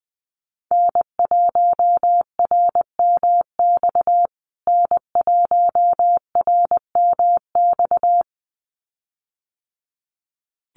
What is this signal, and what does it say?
Morse code, heard as a keyed tone, about 20 wpm, 700 Hz. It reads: N1RMX N1RMX
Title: USA Call Signs Repeated 2x 20wpm